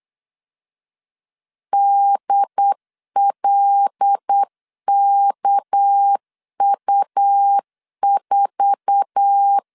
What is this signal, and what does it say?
Morse code, heard as a keyed tone, about 8 wpm, 800 Hz. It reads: DLKU4
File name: LEZIONI / 10 SIGNS / 10D COMA 8wpm 3749